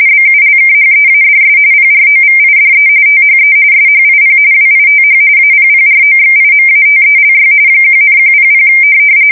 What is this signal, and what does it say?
RTTY, about 45.45 baud. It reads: BULKHEADS PROVIDE THE SHIP WITH APPROXIMATELY 1,100
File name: fsk170 / USS-New-Jersey